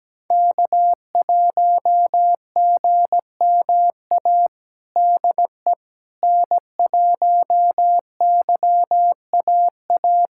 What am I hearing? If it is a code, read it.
Morse code, heard as a keyed tone, about 17 wpm, 700 Hz. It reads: K1GMA DE N1YAA
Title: USA Callsign DE Callsign - 5 Characters - 17wpm